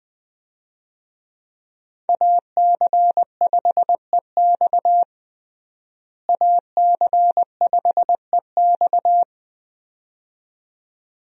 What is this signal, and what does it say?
Morse code, heard as a keyed tone, about 21 wpm, 700 Hz. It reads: AC5EX AC5EX
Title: USA Call Signs - 5 Characters - Repeated 2x - 20wpm